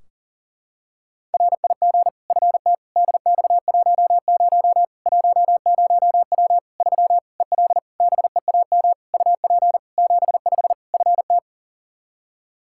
Morse code, heard as a keyed tone, about 40 wpm, 700 Hz. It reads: RIG FT DX10 10W 3 EL BEAM UP 75 FT
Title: K1USN SST QSO Practice 40wpm